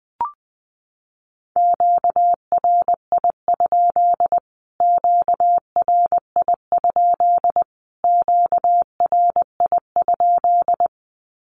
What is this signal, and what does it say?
Morse code, heard as a keyed tone, about 19 wpm, 700 Hz. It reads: QRI? QRI? QRI?
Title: Q-Codes Repeated 3x 20wpm